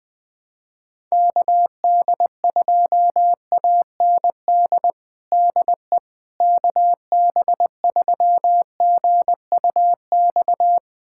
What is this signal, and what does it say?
Morse code, heard as a keyed tone, about 21 wpm, 700 Hz. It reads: KD2AND DE KB3GUX
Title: USA Callsign DE Callsign - 6 Characters - 20wpm